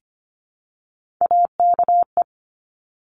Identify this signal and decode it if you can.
Morse code, heard as a keyed tone, about 25 wpm, 700 Hz. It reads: AKE
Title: Top Three Letter Combinations in Words 25wpm